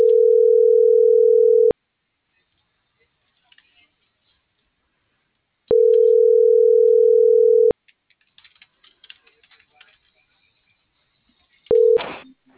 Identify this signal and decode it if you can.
Ringback tone (North America), 440 + 480 Hz.